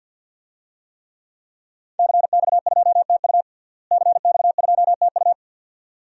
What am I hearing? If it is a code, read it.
Morse code, heard as a keyed tone, about 51 wpm, 700 Hz. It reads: CX2TU CX2TU
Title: DXCC-100 Call Signs - 5 Characters - Repeated 2x - 50wpm